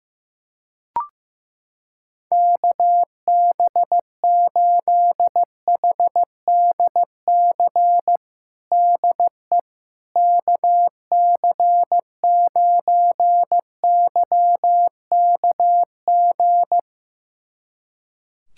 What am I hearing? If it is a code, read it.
Morse code, heard as a keyed tone, about 15 wpm, 700 Hz. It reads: KB8HDC DE KC9YKG
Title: USA Callsign DE Callsign - 6 Characters - 15wpm